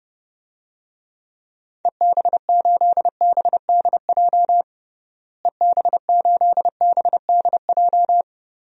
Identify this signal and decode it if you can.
Morse code, heard as a keyed tone, about 32 wpm, 700 Hz. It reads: EB8BDJ EB8BDJ
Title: DXCC-100 Call Signs - 6 Characters - Repeated 3x - 30wpm